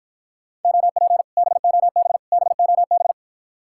Morse code, heard as a keyed tone, about 54 wpm, 700 Hz. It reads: OP BOB BOB